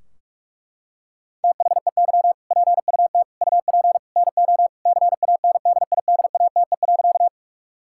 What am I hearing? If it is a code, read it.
Morse code, heard as a keyed tone, about 46 wpm, 700 Hz. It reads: THEY PUT UP NO CANDIDATE.
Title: Sentences from Top 1600 Words 45wpm